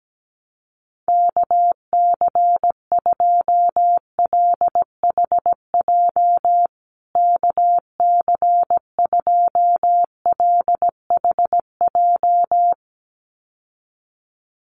Morse code, heard as a keyed tone, about 17 wpm, 700 Hz. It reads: KC2LHJ KC2LHJ
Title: USA Call Signs Repeated 2x 17wpm